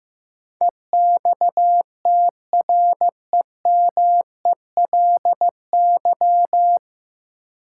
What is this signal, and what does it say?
Morse code, heard as a keyed tone, about 15 wpm, 700 Hz. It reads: EXTREMELY
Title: Sentences from Top 1000 Words - Review Part 7 - New Word Drill - 15wpm